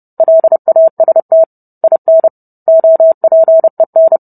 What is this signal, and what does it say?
Morse code, heard as a keyed tone, about 30 wpm, 650 Hz. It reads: LAST IN OPEN